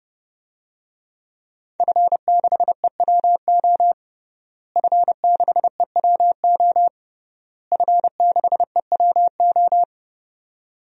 Morse code, heard as a keyed tone, about 30 wpm, 700 Hz. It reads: F6EWO F6EWO F6EWO
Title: DXCC-100 Call Signs - 5 Characters - Repeated 3x - 30wpm